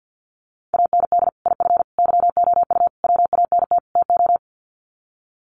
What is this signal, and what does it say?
Morse code, heard as a keyed tone, about 50 wpm, 700 Hz. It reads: AND IF YOU WANT TO
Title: Sets of 5 Words Encore 50wpm